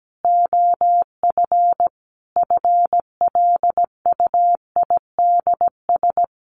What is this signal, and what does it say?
Morse code, heard as a keyed tone, about 17 wpm, 700 Hz. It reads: OF FLUIDS
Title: A Sound Of Thunder 17wpm